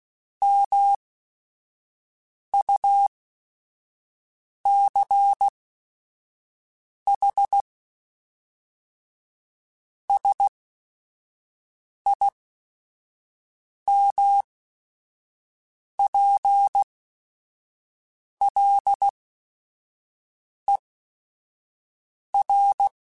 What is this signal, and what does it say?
Morse code, heard as a keyed tone, about 16 wpm, spaced as for 5 wpm, 800 Hz. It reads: MUCH SIMPLER
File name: Technical Subjects / Technical Subjects-mp3s / AND Gate @5